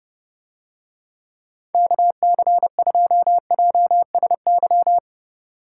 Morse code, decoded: KC2JSY

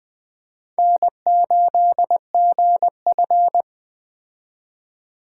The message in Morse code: N8GF

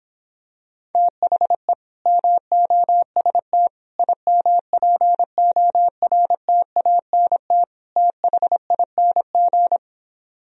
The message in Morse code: THE MOST IMPORTANT THING

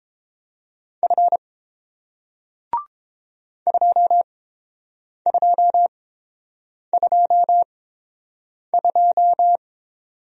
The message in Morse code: F 2222

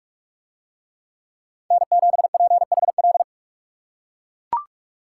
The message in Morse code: N7PHL

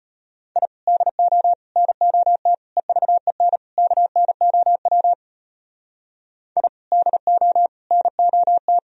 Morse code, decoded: I DO NOT EVEN KNOW I DO NOT

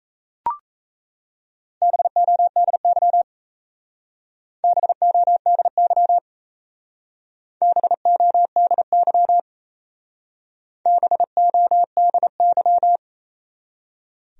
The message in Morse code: BODY BODY BODY BODY

